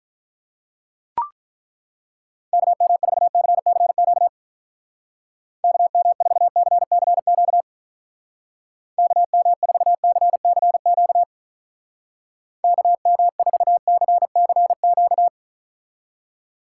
Morse code, heard as a keyed tone, about 43 wpm, 700 Hz. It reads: KM4CCQ KM4CCQ KM4CCQ KM4CCQ
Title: Speed Racing - USA 2x3 Call Signs 35wpm